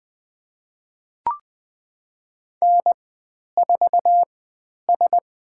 Morse code, heard as a keyed tone, about 20 wpm, 700 Hz. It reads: N 4 S
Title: 102 Lesson 17g - Letter u - Call Signs 20wpm